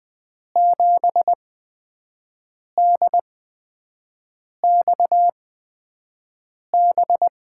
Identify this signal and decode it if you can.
Morse code, heard as a keyed tone, about 20 wpm, spaced as for 5 wpm, 700 Hz. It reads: 7DXB
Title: USA Call Signs - 6 Characters - 8x Character Spacing - 20wpm